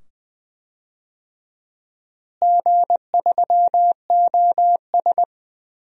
Morse code, heard as a keyed tone, about 20 wpm, 700 Hz. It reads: G3OS
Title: DXCC-100 Call Signs - 4 Characters - Repeated 2x - 20wpm